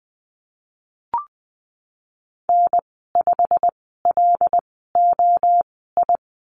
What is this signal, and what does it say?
Morse code, heard as a keyed tone, about 20 wpm, spaced as for 15 wpm, 700 Hz. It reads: N5LOI
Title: 088 Lesson 15g - Number 5 - Call Signs 20wpm